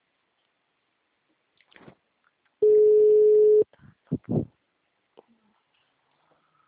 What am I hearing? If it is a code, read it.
Ringback tone (Europe, 425 Hz).